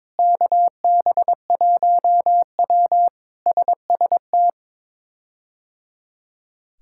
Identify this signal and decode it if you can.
Morse code, heard as a keyed tone, about 22 wpm, 700 Hz. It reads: KB1W SST
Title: K1USN SST QSO Practice - 22wpm